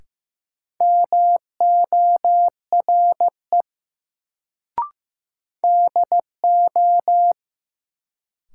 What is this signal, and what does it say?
Morse code, heard as a keyed tone, about 15 wpm, 700 Hz. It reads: MORE DO
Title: Sets of 2 Words - New Word Drill - 15wpm